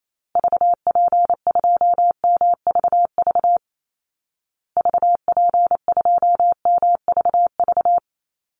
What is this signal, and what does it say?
Morse code, heard as a keyed tone, about 28 wpm, 700 Hz. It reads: VP2MVV VP2MVV
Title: Call Signs Repeated 2x - 28wpm